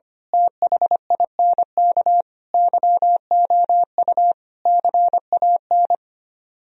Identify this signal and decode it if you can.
Morse code, heard as a keyed tone, about 24 wpm, 700 Hz. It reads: THINK YOU CAN
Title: Speed-Racing - Sets of 5 Words 25wpm